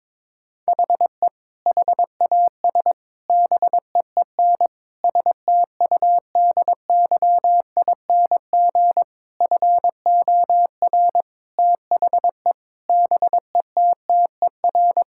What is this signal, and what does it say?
Morse code, heard as a keyed tone, about 21 wpm, 700 Hz. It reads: HE HAS BEEN STUDYING FOR THE BETTER
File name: Sentences with 7-Word Phrases - 22wpm